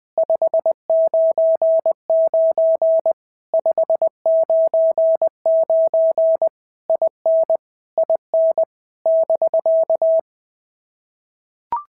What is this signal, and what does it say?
Morse code, heard as a keyed tone, about 20 wpm, 650 Hz. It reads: 599 599 IN IN <BK>
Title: POTA QSO Practice - 20wpm